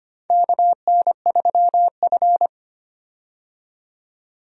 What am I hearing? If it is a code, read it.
Morse code, heard as a keyed tone, about 25 wpm, 700 Hz. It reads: KN3F
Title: USA Call Signs - 4 Characters - 25wpm